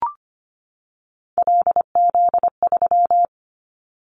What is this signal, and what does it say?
Morse code, heard as a keyed tone, about 25 wpm, 700 Hz. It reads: LZ3
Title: Callsign Prefixes for Top 100 DXCC - 25wpm